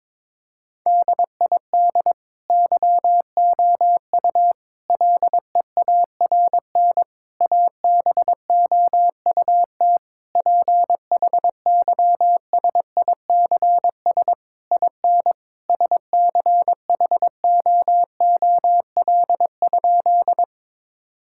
Morse code, heard as a keyed tone, about 22 wpm, 700 Hz. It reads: DID YOU LEARN ABOUT PHYSICS IN SCHOOL?
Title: Sentences from Top 1000 Words - Review Part 5 - 22wpm